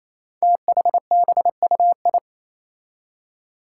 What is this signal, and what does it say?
Morse code, heard as a keyed tone, about 27 wpm, 700 Hz. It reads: THBUI
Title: Five Challenging Letter-Number Combinations 28wpm